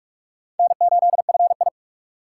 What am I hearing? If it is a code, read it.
Morse code, heard as a keyed tone, about 55 wpm, 700 Hz. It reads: N8FI
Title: Speed Racing - USA 1x2 Call Signs 45wpm